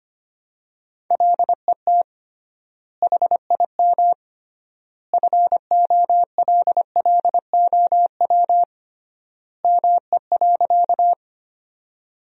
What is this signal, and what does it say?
Morse code, heard as a keyed tone, about 25 wpm, 700 Hz. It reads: LET HIM FOLLOW ME.
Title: 3x Word-Spacing - Sentences from Top 300 Words - 25wpm